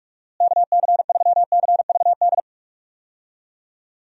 Morse code, heard as a keyed tone, about 46 wpm, 700 Hz. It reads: KC3CVD
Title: USA Call Signs - 6 Characters - Repeated 3x - 45wpm